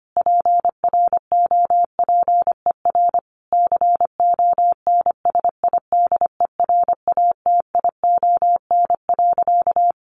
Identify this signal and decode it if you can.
Morse code, heard as a keyed tone, about 25 wpm, 700 Hz. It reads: PROPER CONSIDERATION.
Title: Sentences from Top 1500 Words 25wpm